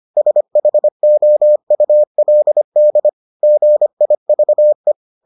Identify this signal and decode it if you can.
Morse code, heard as a keyed tone, about 25 wpm, 600 Hz. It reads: SHOULD GIVE